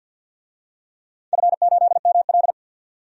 Morse code, heard as a keyed tone, about 49 wpm, 700 Hz. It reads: F8ML
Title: DXCC-100 Call Signs - 4 Characters - 50wpm